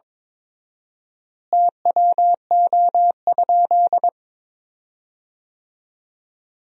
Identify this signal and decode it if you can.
Morse code, heard as a keyed tone, about 22 wpm, 700 Hz. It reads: TWO?
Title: 4x Word-Spacing - Sentences from Top 100 Words - 22wpm